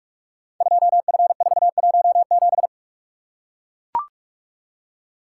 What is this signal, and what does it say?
Morse code, heard as a keyed tone, about 44 wpm, 700 Hz. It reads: 2F417
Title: Code Groups - 5 Letter:Numbers - 45wpm